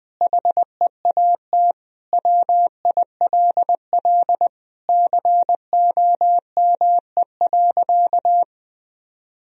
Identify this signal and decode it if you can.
Morse code, heard as a keyed tone, about 20 wpm, 700 Hz. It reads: HEAT WILL COME.
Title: Sentences from Top 1000 Words - Review Part 3 - 20wpm